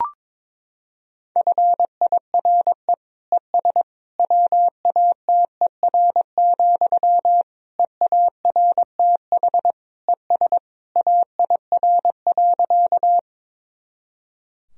Morse code, heard as a keyed tone, about 21 wpm, 700 Hz. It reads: FIRE ES WATER, EARTH ES AIR.
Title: Sentences from Top 400 Words and Abbreviations 22wpm